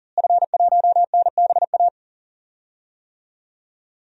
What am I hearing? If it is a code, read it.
Morse code, heard as a keyed tone, about 40 wpm, 700 Hz. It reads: F1NBA
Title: DXCC-100 Call Signs - 5 Characters - Repeated 2x - 40wpm